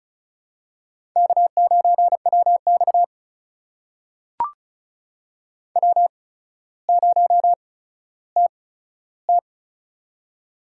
Morse code, heard as a keyed tone, about 33 wpm, spaced as for 15 wpm, 700 Hz. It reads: K9WX W0TT
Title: USA Call Signs - 4 Characters - 8x Character Spacing - 35wpm